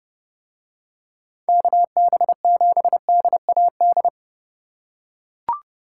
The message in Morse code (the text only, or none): KB7DAD